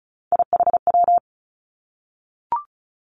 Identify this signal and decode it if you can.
Morse code, heard as a keyed tone, about 36 wpm, 700 Hz. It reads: IHW
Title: Three Challenging Letter-Number Combinations 35wpm